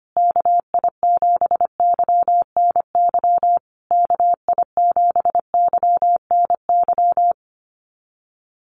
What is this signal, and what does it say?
Morse code, heard as a keyed tone, about 25 wpm, 700 Hz. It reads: KI7YNY KI7YNY
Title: USA Call Signs Repeated 3x 25wpm